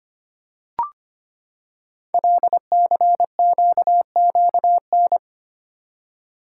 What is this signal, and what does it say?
Morse code, heard as a keyed tone, about 25 wpm, 700 Hz. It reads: LCQQN
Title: Code Groups - 5 Letters - 25wpm